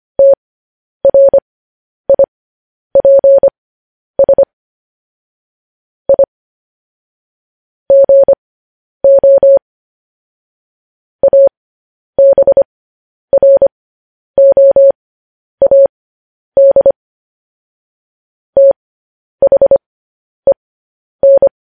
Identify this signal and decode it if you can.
Morse code, heard as a keyed tone, about 25 wpm, spaced as for 10 wpm, 550 Hz. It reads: TRIPS I GO ABROAD THEN
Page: News Headlines in Morse Code at 10 WPM